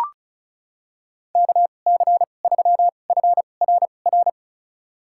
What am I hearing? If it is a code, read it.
Morse code, heard as a keyed tone, about 36 wpm, spaced as for 25 wpm, 700 Hz. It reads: KC3FRR
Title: USA Call Signs - 2x Character Spacing - 35wpm